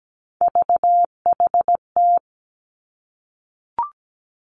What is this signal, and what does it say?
Morse code, heard as a keyed tone, about 17 wpm, 700 Hz. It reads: VHT
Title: Three Challenging Letter-Number Combinations 17wpm